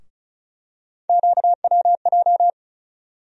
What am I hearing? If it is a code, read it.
Morse code, heard as a keyed tone, about 34 wpm, 700 Hz. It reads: QWJ
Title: USA Call Sign Postfixes - 3 Letters 35wpm